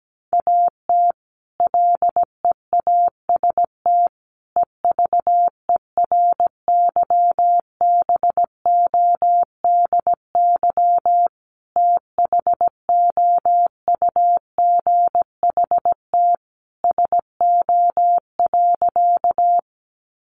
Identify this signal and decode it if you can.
Morse code, heard as a keyed tone, about 17 wpm, 700 Hz. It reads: AT LEAST EVERYBODY THOUGHT SO.
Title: Sentences from Top 1400 Words 17wpm